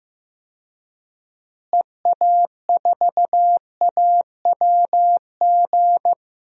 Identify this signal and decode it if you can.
Morse code, heard as a keyed tone, about 15 wpm, 700 Hz. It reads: EA4AWG
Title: DXCC-100 Call Signs - 6 Characters - Repeated 3x - 15wpm